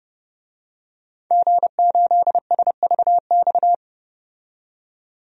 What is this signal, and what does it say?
Morse code, heard as a keyed tone, about 30 wpm, 700 Hz. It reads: G8SVX
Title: DXCC-100 Call Signs - 5 Characters - 30wpm